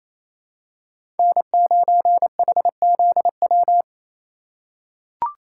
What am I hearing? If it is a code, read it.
Morse code, heard as a keyed tone, about 28 wpm, 700 Hz. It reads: N9HZW
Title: USA Call Signs - 5 Characters - 2x Character Spacing - 28wpm